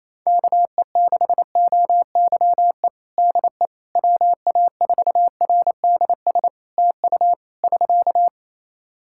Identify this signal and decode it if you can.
Morse code, heard as a keyed tone, about 27 wpm, 700 Hz. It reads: KE6OYE DE WA4RDS TU <SK>
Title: Practice QSO - Exchange 4 - Ending - 28wpm